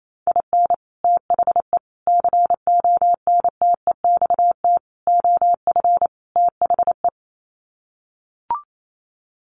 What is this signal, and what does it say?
Morse code, heard as a keyed tone, about 28 wpm, 700 Hz. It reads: IN THE CONTEXT OF THE
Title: Sets of 5 Words 28wpm